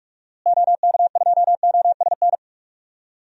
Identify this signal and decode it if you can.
Morse code, heard as a keyed tone, about 45 wpm, 700 Hz. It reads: OK2OSN
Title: DXCC-100 Call Signs - 6 Characters - Repeated 3x - 45wpm